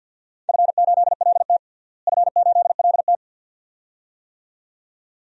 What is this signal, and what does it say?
Morse code, heard as a keyed tone, about 48 wpm, 700 Hz. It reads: F8LT F8LT
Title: DXCC-100 Call Signs - 4 Characters - Repeated 2x - 50wpm